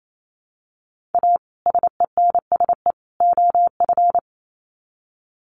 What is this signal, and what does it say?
Morse code, heard as a keyed tone, about 28 wpm, 700 Hz. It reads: A SENSE OF
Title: Sets of 3 Words 28wpm